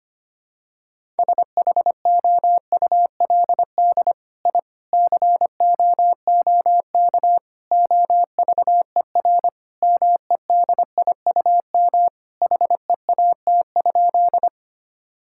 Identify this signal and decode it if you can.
Morse code, heard as a keyed tone, about 25 wpm, 700 Hz. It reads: SHOULD I COOK OVER MEDIUM HEAT?
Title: Sentences from Top 1000 Words - Review Part 6 - 25wpm